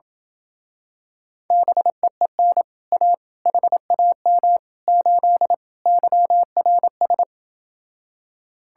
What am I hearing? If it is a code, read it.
Morse code, heard as a keyed tone, about 28 wpm, 700 Hz. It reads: BEEN A HAM 8 YRS